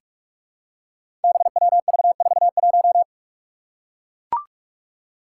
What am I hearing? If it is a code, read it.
Morse code, heard as a keyed tone, about 41 wpm, 700 Hz. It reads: BWV41
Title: Five Challenging Letter-Number Combinations 45wpm